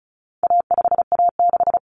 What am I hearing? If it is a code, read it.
Morse code, heard as a keyed tone, about 34 wpm, 700 Hz. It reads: A5A6
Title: Common Cut Numbers - 4-Digit 35wpm